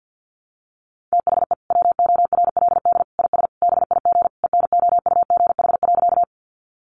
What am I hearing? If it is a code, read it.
Morse code, heard as a keyed tone, about 51 wpm, 700 Hz. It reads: THE WORLD IS BIG ENOUGH.
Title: Sentences from Top 400 Words 50wpm